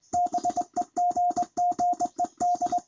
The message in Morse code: BEGGED

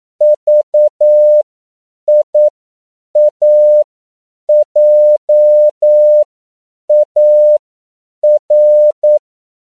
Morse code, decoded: VIAJAR

